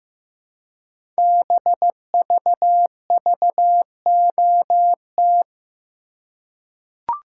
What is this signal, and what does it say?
Morse code, heard as a keyed tone, about 15 wpm, 700 Hz. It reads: BVVOT
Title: Code Groups - 5 Letter:Numbers - 15wpm